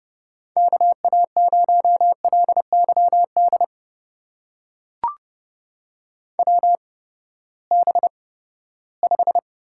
Morse code, heard as a keyed tone, about 30 wpm, spaced as for 17 wpm, 700 Hz. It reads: KA0LYD WB5
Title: USA Call Signs - 6 Characters - 8x Character Spacing - 30wpm